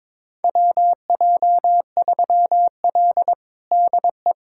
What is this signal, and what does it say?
Morse code, heard as a keyed tone, about 22 wpm, 700 Hz. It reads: WJ3L DE
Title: USA Callsign DE Callsign - 4 Characters - 22wpm